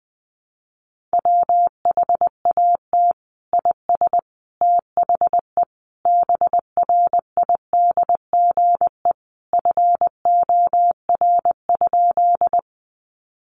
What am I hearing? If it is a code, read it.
Morse code, heard as a keyed tone, about 20 wpm, 700 Hz. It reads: WHAT IS THE BRIDGE FOR?
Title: Sentences from Top 1200 Words 20wpm